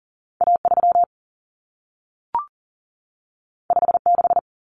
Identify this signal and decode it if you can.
Morse code, heard as a keyed tone, about 40 wpm, 700 Hz. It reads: A3 56
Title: Common Cut Numbers - 2-Digit 40wpm